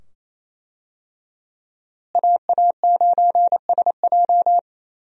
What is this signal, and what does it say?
Morse code, heard as a keyed tone, about 28 wpm, 700 Hz. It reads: AA9SJ
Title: USA Call Signs - 5 Characters - Repeated 2x - 28wpm